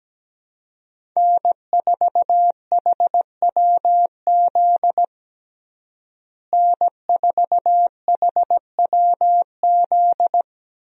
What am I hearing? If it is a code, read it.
Morse code, heard as a keyed tone, about 17 wpm, 700 Hz. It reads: N4HWZ N4HWZ
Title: USA Call Signs - 5 Characters - Repeated 2x - 17wpm